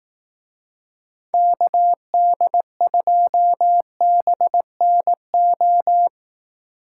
Morse code, heard as a keyed tone, about 18 wpm, 700 Hz. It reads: KD2BNO
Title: Speed Racing - USA 2x3 Call Signs 15wpm